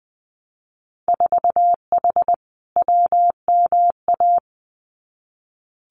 Morse code, heard as a keyed tone, about 20 wpm, 700 Hz. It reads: 4H WMA
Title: Winter Field Day Exchanges Encore 20wpm